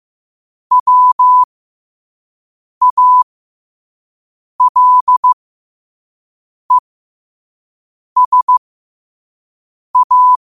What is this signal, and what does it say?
Morse code, heard as a keyed tone, about 15 wpm, spaced as for 5 wpm, 1000 Hz. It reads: WALESA
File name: qotdfiles / 5wpm